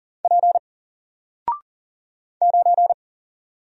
Morse code, heard as a keyed tone, about 40 wpm, 700 Hz. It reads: P 9
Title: Single Character (Letter:Number:Punctuation:Prosign) 40wpm